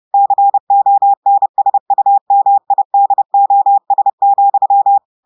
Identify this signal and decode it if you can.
Morse code, heard as a keyed tone, about 30 wpm, 800 Hz. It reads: CONSUMIDOS,
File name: portugais texte en clair 2 vitesse 30 mots minutes